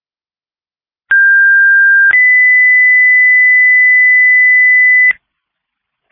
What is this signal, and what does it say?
Two-tone page, heard as a one-second tone, then a three-second tone, about 1600 then 2000 Hz. It is heard from 1.1 to 5.1 s.